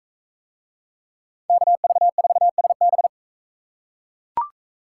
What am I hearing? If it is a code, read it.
Morse code, heard as a keyed tone, about 39 wpm, 700 Hz. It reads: KV4SB